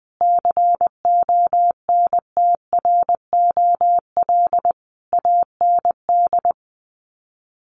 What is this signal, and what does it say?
Morse code, heard as a keyed tone, about 20 wpm, 700 Hz. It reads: CONTROL AND
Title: Sets of 5 Words Encore2 20wpm